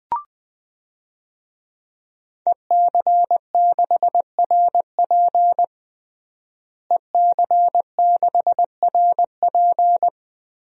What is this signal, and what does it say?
Morse code, heard as a keyed tone, about 20 wpm, 700 Hz. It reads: EC6RP EC6RP
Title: DXCC-100 Call Signs - 5 Characters - Repeated 3x - 20wpm